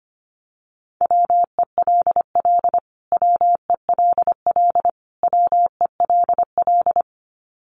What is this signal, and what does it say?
Morse code, heard as a keyed tone, about 25 wpm, 700 Hz. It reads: WELL WELL WELL
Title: Top 300 Words Repeated 25wpm